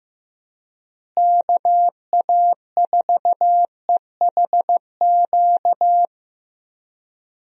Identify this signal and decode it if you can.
Morse code, heard as a keyed tone, about 15 wpm, 700 Hz. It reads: KA4EHQ
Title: USA Call Signs - 6 Characters - Repeated 3x - 15wpm